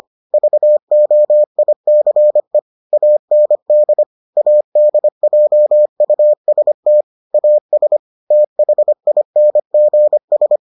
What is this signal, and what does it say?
Morse code, heard as a keyed tone, about 25 wpm, 600 Hz. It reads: VOICE AND ADJUST AS THINGS